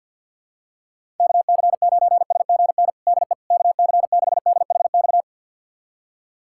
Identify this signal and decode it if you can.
Morse code, heard as a keyed tone, about 52 wpm, 700 Hz. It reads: KC9SGN DE KC6DHX
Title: USA Callsign DE Callsign 50wpm